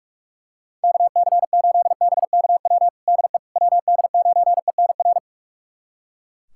Morse code, heard as a keyed tone, about 42 wpm, 700 Hz. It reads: KC8BKW DE WD9ENR
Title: USA Callsign DE Callsign - 6 Characters - 45wpm